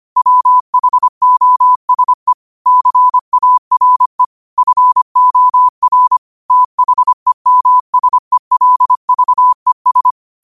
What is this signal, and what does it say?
Morse code, heard as a keyed tone, about 25 wpm, 1000 Hz. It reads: WHOSE CARE FOR THEMSELVES